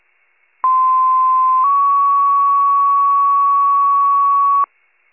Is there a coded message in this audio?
Yes: two-tone page from 0.6 to 4.6 s, heard as a one-second tone, then a three-second tone, about 1040 then 1120 Hz.